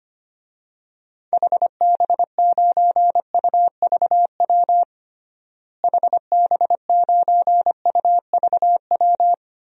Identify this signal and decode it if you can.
Morse code, heard as a keyed tone, about 25 wpm, 700 Hz. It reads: HB9UVW HB9UVW
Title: DXCC-100 Call Signs - 6 Characters - Repeated 2x - 25wpm